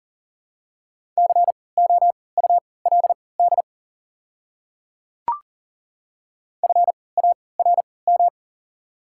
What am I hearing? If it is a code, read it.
Morse code, heard as a keyed tone, about 40 wpm, spaced as for 23 wpm, 690 Hz. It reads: COULD FARM
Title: 3x Character Spacing - Top 500 Words - 40wpm